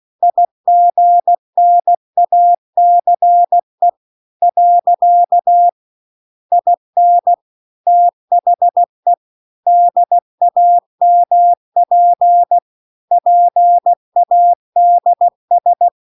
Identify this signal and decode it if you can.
Morse code, heard as a keyed tone, about 16 wpm, 700 Hz. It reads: IGNACE . IN THE DAMP PADS
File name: CW-The-Big-Twoheaded-River-2-16WPM